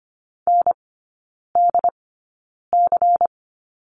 Morse code, heard as a keyed tone, about 25 wpm, spaced as for 9 wpm, 700 Hz. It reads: NDC
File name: course / 25:10wpm / 093 Lesson 16e - Letter c - Three Characters 25wpm